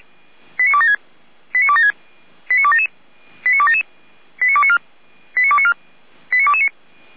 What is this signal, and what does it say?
Five-tone selective call (ZVEI), heard as quick runs of five tones, tones of about 70 ms. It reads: 89276 89276 89277 89277 89294 89294 89229